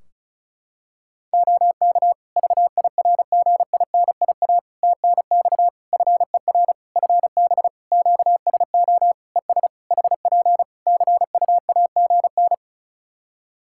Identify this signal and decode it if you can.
Morse code, heard as a keyed tone, about 34 wpm, 700 Hz. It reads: OK VIRGINIA TNX FER FB QSO ES HP CUAGN
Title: Practice QSO - Exchange 4 - Ending - 35wpm